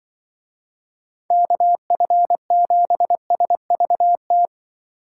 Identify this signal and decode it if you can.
Morse code, heard as a keyed tone, about 23 wpm, 700 Hz. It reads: KF7SVT
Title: Speed Racing - USA 2x3 Call Signs 20wpm